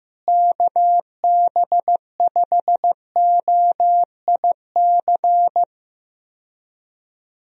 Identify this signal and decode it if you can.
Morse code, heard as a keyed tone, about 15 wpm, 700 Hz. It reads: KB5OIC
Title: USA Call Signs - 6 Characters - Repeated 2x - 15wpm